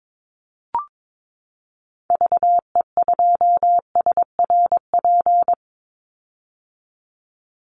Morse code, heard as a keyed tone, about 22 wpm, 700 Hz. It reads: VE2SRP
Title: Call Signs - 22wpm